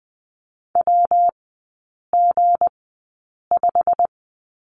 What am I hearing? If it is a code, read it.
Morse code, heard as a keyed tone, about 20 wpm, spaced as for 8 wpm, 700 Hz. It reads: WG5
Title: 159 Lesson 26e - Letter g - Three Characters 20wpm